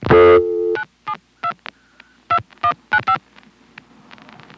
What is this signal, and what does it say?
DTMF, heard as telephone touch tones, tones of about 90 ms. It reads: 972 2195